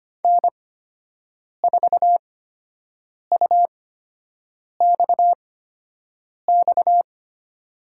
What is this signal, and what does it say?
Morse code, heard as a keyed tone, about 25 wpm, spaced as for 7 wpm, 700 Hz. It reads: N4UXX
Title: USA Call Signs - 5 Characters - 8x Character Spacing - 25wpm